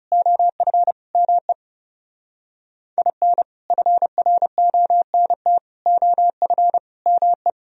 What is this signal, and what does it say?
Morse code, heard as a keyed tone, about 32 wpm, 700 Hz. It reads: OF ME IN FRONT OF ME
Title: Speed-Racing - Sets of 4 Words 30wpm